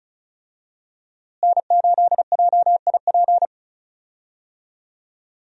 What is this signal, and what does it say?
Morse code, heard as a keyed tone, about 36 wpm, 700 Hz. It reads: N8JIP
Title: USA Call Signs - 5 Characters - Repeated 2x - 35wpm